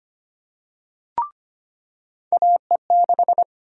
Morse code, heard as a keyed tone, about 25 wpm, 700 Hz. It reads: AE6
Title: USA Call Sign Prefixes - 25wpm